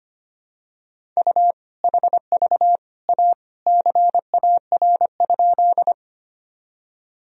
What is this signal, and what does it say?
Morse code, heard as a keyed tone, about 25 wpm, 700 Hz. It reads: U HV A CAR?
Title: Sentences from Top 300 Words and Abbreviations 25wpm